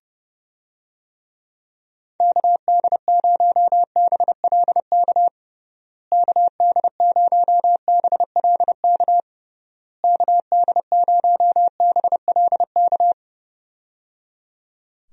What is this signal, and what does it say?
Morse code, heard as a keyed tone, about 30 wpm, 700 Hz. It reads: KD0BLK KD0BLK KD0BLK
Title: USA Call Signs - 6 Characters - Repeated 3x - 30wpm